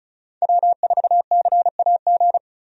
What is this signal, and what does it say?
Morse code, heard as a keyed tone, about 34 wpm, 700 Hz. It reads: W4CAG
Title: USA Call Signs - 5 Characters - Repeated 2x - 35wpm